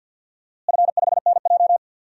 Morse code, heard as a keyed tone, about 47 wpm, 700 Hz. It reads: F5NJ
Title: DXCC-100 Call Signs - 4 Characters - Repeated 3x - 50wpm